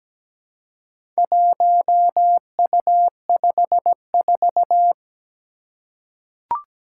Morse code, heard as a keyed tone, about 17 wpm, 700 Hz. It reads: 1U54
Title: Four Challenging Letter-Number Combinations 17wpm